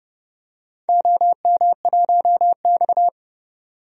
Morse code, heard as a keyed tone, about 31 wpm, 700 Hz. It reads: OM1X
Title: DXCC-100 Call Signs - 4 Characters - 30wpm